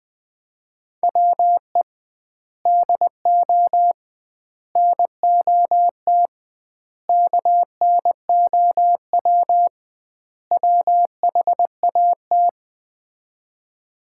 Morse code, heard as a keyed tone, about 20 wpm, 700 Hz. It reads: WE DO NOT KNOW WHAT
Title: 2x Word-Spacing - Sets of 5 Words - 20wpm